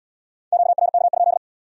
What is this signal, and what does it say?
Morse code, heard as a keyed tone, about 72 wpm, 700 Hz. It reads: QRK?